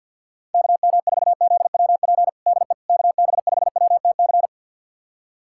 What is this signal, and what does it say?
Morse code, heard as a keyed tone, about 50 wpm, 690 Hz. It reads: KM4ZWP DE KB5WTC
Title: USA Callsign DE Callsign 50wpm